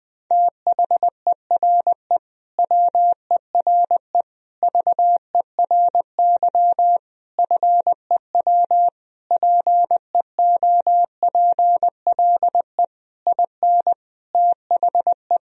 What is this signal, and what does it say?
Morse code, heard as a keyed tone, about 20 wpm, 700 Hz. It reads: THERE WERE VERY FEW PEOPLE IN THE
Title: Sentences from Top 1000 Words - Review Part 2 - 20wpm